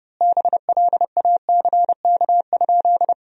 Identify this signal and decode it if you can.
Morse code, heard as a keyed tone, about 30 wpm, 700 Hz. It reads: BLACK?
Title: 3x Word-Spacing - Sentences from Top 400 Words - 30wpm